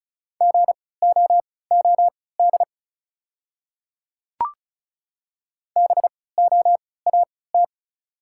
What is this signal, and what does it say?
Morse code, heard as a keyed tone, about 36 wpm, spaced as for 20 wpm, 690 Hz. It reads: GOOD BOAT
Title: 3x Character Spacing - Top 500 Words - 35wpm